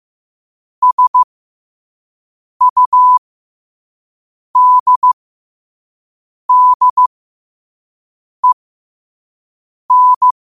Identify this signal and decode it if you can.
Morse code, heard as a keyed tone, about 15 wpm, spaced as for 5 wpm, 1000 Hz. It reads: SUDDEN
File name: 5wpm